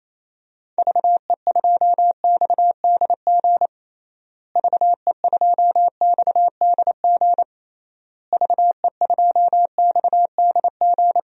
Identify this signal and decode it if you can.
Morse code, heard as a keyed tone, about 28 wpm, 700 Hz. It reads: VE2XDG VE2XDG VE2XDG
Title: DXCC-100 Call Signs - 6 Characters - Repeated 3x - 28wpm